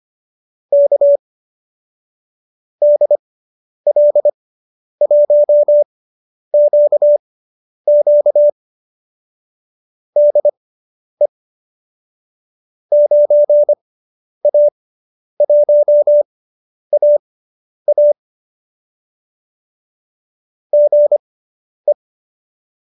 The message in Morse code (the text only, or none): K DL1QQ DE 9A1AA GE